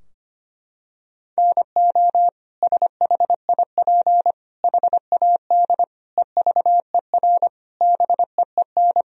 Morse code, heard as a keyed tone, about 25 wpm, 700 Hz. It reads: NO SHIP HAD EVER BEEN
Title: Sentences from Top 1000 Words - Review Part 4 - 25wpm